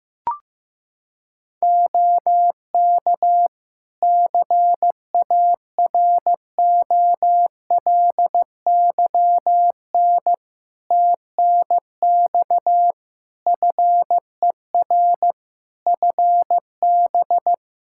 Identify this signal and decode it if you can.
Morse code, heard as a keyed tone, about 15 wpm, 700 Hz. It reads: OK CAROLYN TNX FER FB
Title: Practice QSO - Exchange 4 - Ending - 15wpm